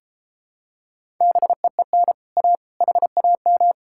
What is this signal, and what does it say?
Morse code, heard as a keyed tone, about 33 wpm, 700 Hz. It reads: BEEN A HAM